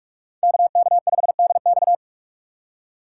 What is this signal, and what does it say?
Morse code, heard as a keyed tone, about 40 wpm, 700 Hz. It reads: KK5DX